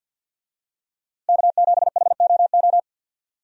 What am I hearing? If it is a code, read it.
Morse code, heard as a keyed tone, about 52 wpm, 700 Hz. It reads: K7HOO
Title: USA Call Signs - 5 Characters - Repeated 3x - 50wpm